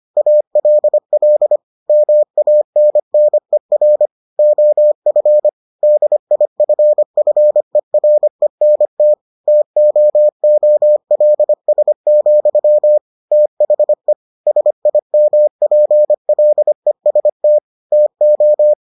Morse code, heard as a keyed tone, about 25 wpm, 600 Hz. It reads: ALL MANNER OF DIFFERENT TOOLS, THE SIMPLEST TO